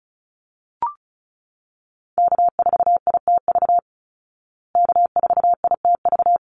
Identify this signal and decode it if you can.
Morse code, heard as a keyed tone, about 34 wpm, 700 Hz. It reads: K4ITV K4ITV
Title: Call Signs Repeated 3x - 35wpm